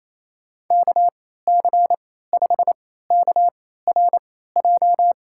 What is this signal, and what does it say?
Morse code, heard as a keyed tone, about 28 wpm, spaced as for 16 wpm, 700 Hz. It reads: KC5KRJ
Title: USA Call Signs - 6 Characters - 3x Character Spacing - 28wpm